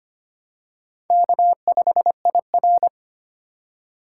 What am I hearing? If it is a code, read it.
Morse code, heard as a keyed tone, about 25 wpm, 700 Hz. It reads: K5IR
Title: USA Call Signs - 8x Character Spacing - 25wpm